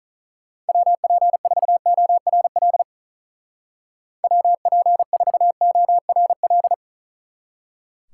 Morse code, heard as a keyed tone, about 44 wpm, 700 Hz. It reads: WP4ORL WP4ORL